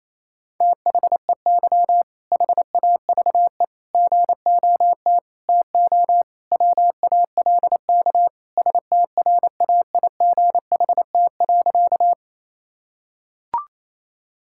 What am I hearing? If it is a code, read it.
Morse code, heard as a keyed tone, about 28 wpm, 700 Hz. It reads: THEY HAVE GOT TO WALK STRAIGHT.
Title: Sentences from Top 1000 Words - Review Part 6 - 28wpm